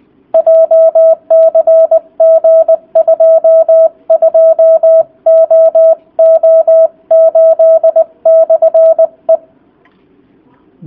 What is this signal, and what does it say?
Morse code, heard as a keyed tone, about 20 wpm, 660 Hz. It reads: JCG 22OO8/E